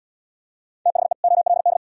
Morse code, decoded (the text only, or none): THE OLD